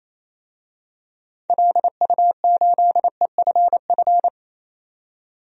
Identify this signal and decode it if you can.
Morse code, heard as a keyed tone, about 27 wpm, 700 Hz. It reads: LU8EFF
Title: DXCC-100 Call Signs - 6 Characters - 28wpm